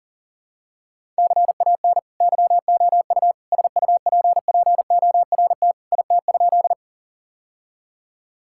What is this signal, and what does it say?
Morse code, heard as a keyed tone, about 40 wpm, 700 Hz. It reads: CAN YOU SUPPORT IT?
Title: Sentences from Top 1000 Words - Review Part 5 - 40wpm